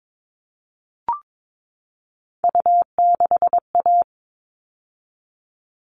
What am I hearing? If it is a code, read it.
Morse code, heard as a keyed tone, about 22 wpm, 700 Hz. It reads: U6A
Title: All Cut Numbers - 3-Digit 22wpm